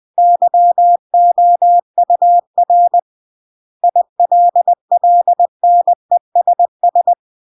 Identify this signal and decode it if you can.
Morse code, heard as a keyed tone, about 20 wpm, 700 Hz. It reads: YOUR ILLNESS
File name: CW / jodorowsky / 82 commandments 20-20